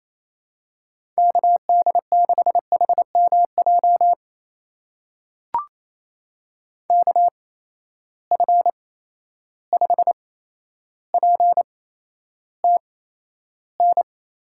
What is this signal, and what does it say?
Morse code, heard as a keyed tone, about 28 wpm, spaced as for 12 wpm, 700 Hz. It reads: KD6HMJ KF5PTN
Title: USA Call Signs - 6 Characters - 8x Character Spacing - 28wpm